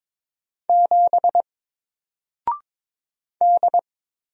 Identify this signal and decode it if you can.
Morse code, heard as a keyed tone, about 23 wpm, 700 Hz. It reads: 7 D
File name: Single-Letter-Number-22wpm